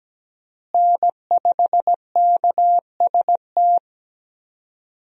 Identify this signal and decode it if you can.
Morse code, heard as a keyed tone, about 17 wpm, 700 Hz. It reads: N5KST
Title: USA Call Signs - 5 Characters - Repeated 2x - 17wpm